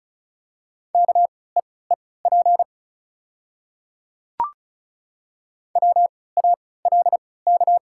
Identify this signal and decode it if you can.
Morse code, heard as a keyed tone, about 35 wpm, spaced as for 20 wpm, 700 Hz. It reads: KEEP WALK
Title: 3x Character Spacing - Top 500 Words - 35wpm